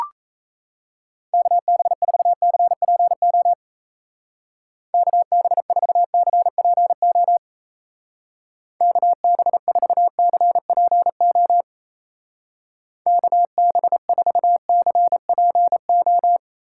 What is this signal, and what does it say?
Morse code, heard as a keyed tone, about 35 wpm, 700 Hz. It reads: KB4CPO KB4CPO KB4CPO KB4CPO
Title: Speed Racing - USA 2x3 Call Signs 28wpm